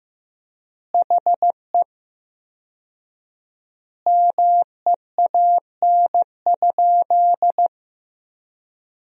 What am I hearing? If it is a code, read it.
Morse code, heard as a keyed tone, about 16 wpm, 700 Hz. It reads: HE MEAN?
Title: 4x Word-Spacing - Sentences from Top 200 Words - 15wpm